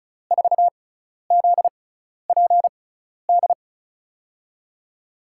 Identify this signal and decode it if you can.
Morse code, heard as a keyed tone, about 34 wpm, spaced as for 12 wpm, 700 Hz. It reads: 4ZPD
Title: USA Call Signs - 6 Characters - 6x Character Spacing - 35wpm